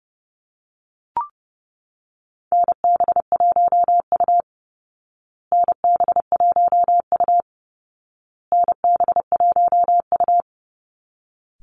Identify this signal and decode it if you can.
Morse code, heard as a keyed tone, about 30 wpm, 700 Hz. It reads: NB1U NB1U NB1U
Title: Call Signs Repeated 2x - 30wpm